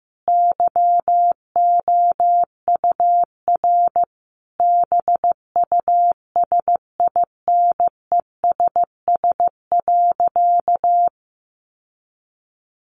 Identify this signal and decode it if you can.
Morse code, heard as a keyed tone, about 15 wpm, 700 Hz. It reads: YOUR BUSINESS.
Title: Sentences from Top 600 Words 15wpm